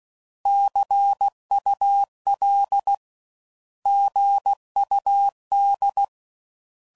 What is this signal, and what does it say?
Morse code, heard as a keyed tone, about 16 wpm, 800 Hz. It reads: CUL GUD
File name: Mixed / Abreviations / Abreviations